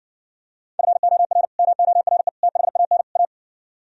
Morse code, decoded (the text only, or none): FOR MORE THAN A